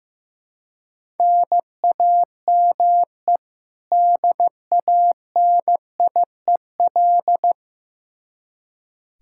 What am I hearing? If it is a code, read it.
Morse code, heard as a keyed tone, about 15 wpm, 700 Hz. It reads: NAME DANIEL
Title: Practice QSO - Exchange 1 - Beginning - 15wpm